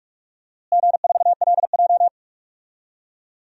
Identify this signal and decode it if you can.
Morse code, heard as a keyed tone, about 46 wpm, 700 Hz. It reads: G4LJ
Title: DXCC-100 Call Signs - 4 Characters - Repeated 3x - 45wpm